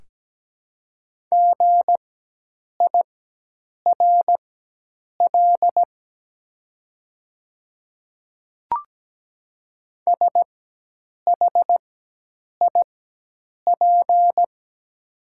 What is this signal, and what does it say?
Morse code, heard as a keyed tone, about 17 wpm, spaced as for 8 wpm, 700 Hz. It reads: GIRL SHIP
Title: 4x Character Spacing - Top 500 Words - 17wpm